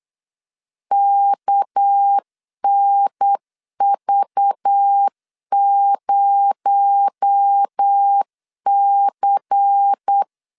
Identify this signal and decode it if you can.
Morse code, heard as a keyed tone, about 8 wpm, 800 Hz. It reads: KNV0C
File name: LEZIONI / 7 GF9Z / 7 GF 8wpm 3108